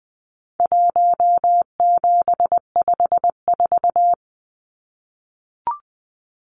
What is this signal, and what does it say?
Morse code, heard as a keyed tone, about 20 wpm, 700 Hz. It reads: 1754
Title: Numbers - Contest Style - 4-Digit 20wpm